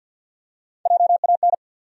Morse code, heard as a keyed tone, about 49 wpm, 700 Hz. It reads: JAN
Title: USA Common Names - Max 3 Letters - 50wpm